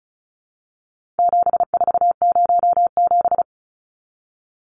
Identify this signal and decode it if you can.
Morse code, heard as a keyed tone, about 34 wpm, 700 Hz. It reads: 7407
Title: Numbers - 4-Digit 35wpm